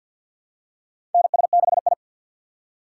Morse code, heard as a keyed tone, about 52 wpm, 700 Hz. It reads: NS6I
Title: USA Call Signs - 4 Characters - 2x Character Spacing - 50wpm